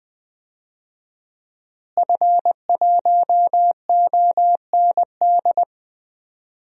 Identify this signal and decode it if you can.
Morse code, heard as a keyed tone, about 20 wpm, 700 Hz. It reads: F1OND
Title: DXCC-100 Call Signs - 5 Characters - Repeated 2x - 20wpm